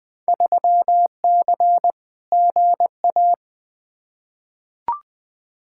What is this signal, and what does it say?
Morse code, heard as a keyed tone, about 20 wpm, 700 Hz. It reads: 3C GA
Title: Field Day Exchanges Encore - 20wpm